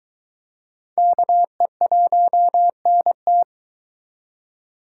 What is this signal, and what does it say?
Morse code, heard as a keyed tone, about 23 wpm, 700 Hz. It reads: KE1NT